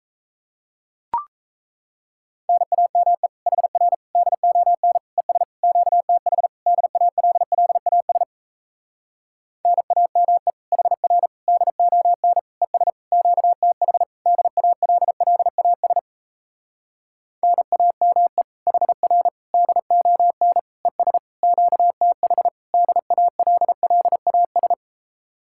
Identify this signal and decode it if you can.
Morse code, heard as a keyed tone, about 38 wpm, 700 Hz. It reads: NAME HR DON ES QTH DALLAS NAME HR DON ES QTH DALLAS NAME HR DON ES QTH DALLAS